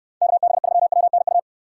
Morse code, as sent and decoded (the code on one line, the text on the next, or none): -.- -... ...-- .-- -. ..-
KB3WNU